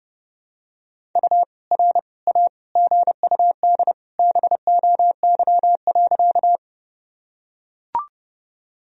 Morse code, heard as a keyed tone, about 30 wpm, 700 Hz. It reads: U R A GUD BOY.
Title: Sentences from Top 200 Words and Abbreviations 30wpm